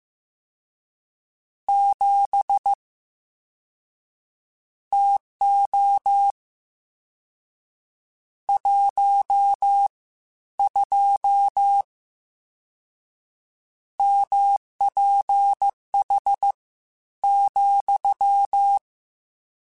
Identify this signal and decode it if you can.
Morse code, heard as a keyed tone, about 15 wpm, spaced as for 11 wpm, 800 Hz. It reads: 7 TO 12 MPH,
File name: Spaced Characters / Spaced Characters-mp3s / Weather5 22 sp @13